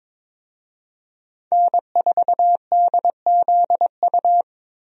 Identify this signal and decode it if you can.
Morse code, heard as a keyed tone, about 22 wpm, 700 Hz. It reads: N4DZU
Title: USA Call Signs - 5 Characters - Repeated 2x - 22wpm